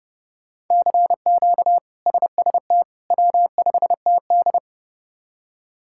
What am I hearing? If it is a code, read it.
Morse code, heard as a keyed tone, about 30 wpm, 700 Hz. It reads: CQ SST W5TD
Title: K1USN SST QSO Practice - 30wpm